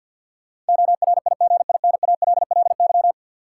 Morse code, heard as a keyed tone, about 49 wpm, 700 Hz. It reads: ORIGINALLY
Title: Sentences from Top 1500 Words - New Word Drill Alt - 50wpm